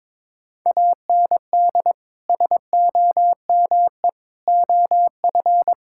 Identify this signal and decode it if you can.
Morse code, heard as a keyed tone, about 22 wpm, 700 Hz. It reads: AND SOME OF
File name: practice / 22wpm / Sentences with 4-Word Phrases - 22wpm